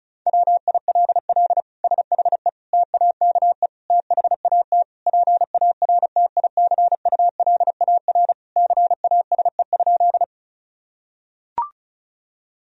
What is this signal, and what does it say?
Morse code, heard as a keyed tone, about 36 wpm, 700 Hz. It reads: WILL SHE TAKE THAT PARTICULAR CASE?
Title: Sentences from Top 1000 Words - Review Part 4 - 35wpm